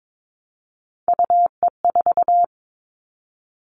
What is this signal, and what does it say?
Morse code, heard as a keyed tone, about 22 wpm, 700 Hz. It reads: UE4
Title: Three Challenging Letter-Number Combinations 22wpm